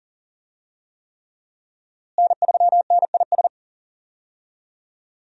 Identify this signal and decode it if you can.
Morse code, heard as a keyed tone, about 40 wpm, 700 Hz. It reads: N3NIS
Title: USA Call Signs - 5 Characters - Repeated 3x - 40wpm